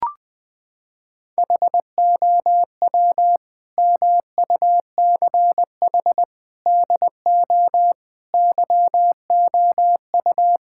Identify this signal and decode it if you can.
Morse code, heard as a keyed tone, about 21 wpm, 700 Hz. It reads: HOW MUCH DO YOU
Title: Sentences from Top 1000 Words - Review Part 6 - 20wpm